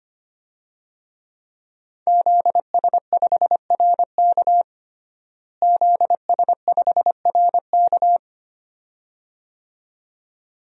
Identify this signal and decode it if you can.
Morse code, heard as a keyed tone, about 25 wpm, 700 Hz. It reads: ZS5RK ZS5RK
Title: DXCC-100 Call Signs - 5 Characters - Repeated 2x - 25wpm